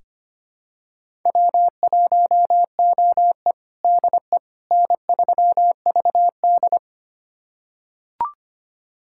Morse code, heard as a keyed tone, about 25 wpm, 700 Hz. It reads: W1OE DE N3VD
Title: USA Callsign DE Callsign - 4 Characters - 25wpm